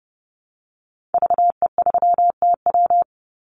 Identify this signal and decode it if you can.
Morse code, heard as a keyed tone, about 30 wpm, 700 Hz. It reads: VE3TW
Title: Call Signs Repeated 3x - 30wpm